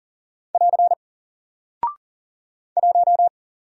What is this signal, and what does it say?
Morse code, heard as a keyed tone, about 40 wpm, 700 Hz. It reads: <AR> 1
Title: Single Character (Letter:Number:Punctuation:Prosign) 40wpm